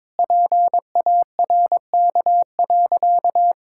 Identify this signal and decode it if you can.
Morse code, heard as a keyed tone, about 22 wpm, 700 Hz. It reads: PARK.
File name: practice / 22wpm / Easy Sentences with Hints - 22wpm